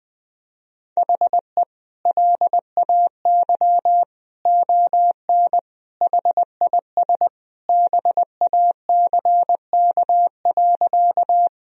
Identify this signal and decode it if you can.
Morse code, heard as a keyed tone, about 20 wpm, 700 Hz. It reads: HE LAY ON HIS BACK.
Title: Sentences from Top 1000 Words - Review Part 2 - 20wpm